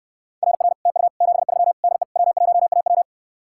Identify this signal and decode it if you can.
Morse code, heard as a keyed tone, about 67 wpm, 700 Hz. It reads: RR TU 73 DE W1AW